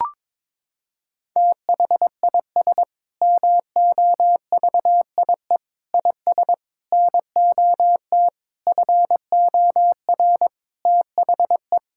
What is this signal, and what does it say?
Morse code, heard as a keyed tone, about 22 wpm, 700 Hz. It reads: THIS MOVIE IS NOT FOR THE
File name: Sentences with 7-Word Phrases - 22wpm